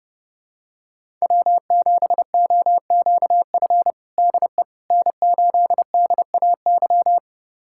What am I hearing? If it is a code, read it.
Morse code, heard as a keyed tone, about 30 wpm, 700 Hz. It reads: W7OQF DE N8DAY
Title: USA Callsign DE Callsign - 5 Characters - 30wpm